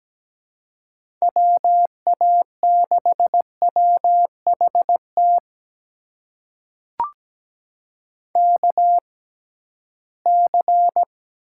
Morse code, heard as a keyed tone, about 17 wpm, spaced as for 13 wpm, 700 Hz. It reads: WA6WHT KC
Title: USA Call Signs - 6 Characters - 6x Character Spacing - 17wpm